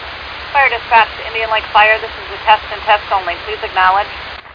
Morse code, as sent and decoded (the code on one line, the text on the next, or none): none